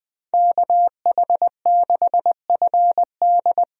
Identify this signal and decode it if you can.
Morse code, heard as a keyed tone, about 20 wpm, 700 Hz. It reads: KH6FD
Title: USA Call Signs - 5 Characters - 20wpm